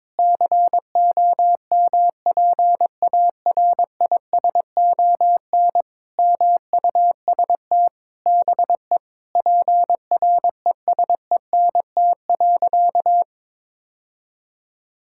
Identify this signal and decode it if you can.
Morse code, heard as a keyed tone, about 22 wpm, 700 Hz. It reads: COMPARISON MUST BE PRESENT.
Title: Sentences from Top 1600 Words 22wpm